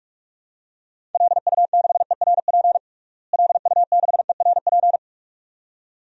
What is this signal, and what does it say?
Morse code, heard as a keyed tone, about 46 wpm, 700 Hz. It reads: LU6ERP LU6ERP
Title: DXCC-100 Call Signs - 6 Characters - Repeated 2x - 45wpm